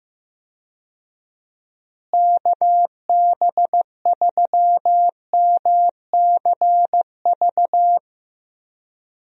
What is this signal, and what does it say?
Morse code, heard as a keyed tone, about 15 wpm, 700 Hz. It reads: KB3MCV
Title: USA Call Signs - 6 Characters - Repeated 2x - 15wpm